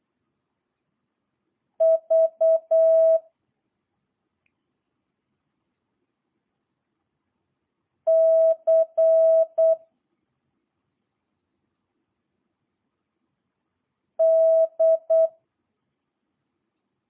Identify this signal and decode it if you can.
Morse code, heard as a keyed tone, about 8 wpm, 650 Hz. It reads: V C D